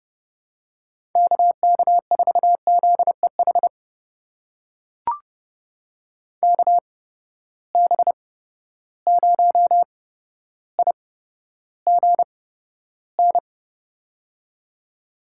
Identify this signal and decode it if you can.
Morse code, heard as a keyed tone, about 30 wpm, spaced as for 13 wpm, 700 Hz. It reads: KK4ZEH KB0IGN
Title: USA Call Signs - 6 Characters - 8x Character Spacing - 30wpm